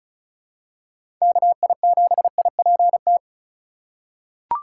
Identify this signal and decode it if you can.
Morse code, heard as a keyed tone, about 36 wpm, 700 Hz. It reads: KI7IPT
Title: USA Call Signs - 6 Characters - 4x Character Spacing - 35wpm